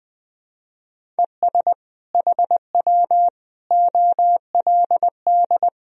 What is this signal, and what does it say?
Morse code, heard as a keyed tone, about 20 wpm, 700 Hz. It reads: ES HW OLD
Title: Sentences from Top 200 Words and Abbreviations 20wpm